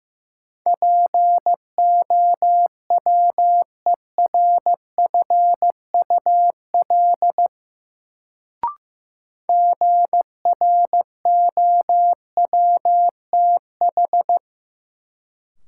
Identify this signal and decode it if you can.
Morse code, heard as a keyed tone, about 15 wpm, 700 Hz. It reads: POWERFUL GROWTH
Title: Sentences from Top 1300 Words - New Word Drill Alt - 15wpm